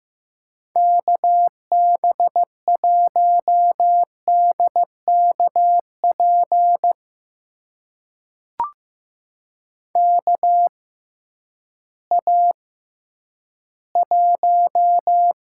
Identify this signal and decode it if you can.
Morse code, heard as a keyed tone, about 15 wpm, spaced as for 10 wpm, 700 Hz. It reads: KB1DKP KA1
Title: USA Call Signs - 6 Characters - 6x Character Spacing - 15wpm